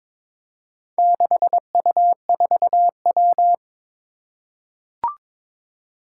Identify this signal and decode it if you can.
Morse code, heard as a keyed tone, about 22 wpm, 700 Hz. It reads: 6U4W
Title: Four Challenging Letter-Number Combinations 22wpm